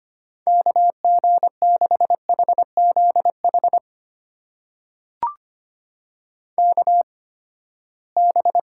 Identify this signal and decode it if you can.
Morse code, heard as a keyed tone, about 25 wpm, spaced as for 17 wpm, 700 Hz. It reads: KG6HZH KB
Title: USA Call Signs - 6 Characters - 8x Character Spacing - 25wpm